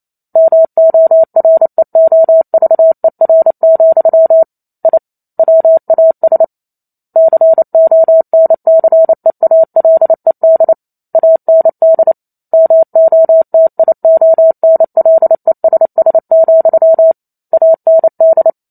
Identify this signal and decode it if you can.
Morse code, heard as a keyed tone, about 28 wpm, 650 Hz. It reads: MOREOVER, I WAS CONCEALED AND MOTIONLESS, AND
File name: Chapter 2 - 28 WPM